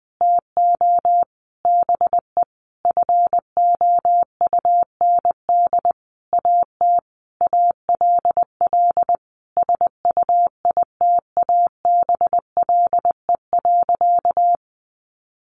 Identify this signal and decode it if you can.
Morse code, heard as a keyed tone, about 20 wpm, 700 Hz. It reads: TO BE FOUND AT ALL SUITABLE.
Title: Sentences from Top 1400 Words 20wpm